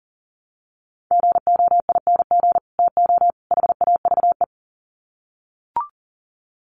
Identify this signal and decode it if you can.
Morse code, heard as a keyed tone, about 40 wpm, 700 Hz. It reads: GOING TO HAVE
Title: Sets of 3 Words 40wpm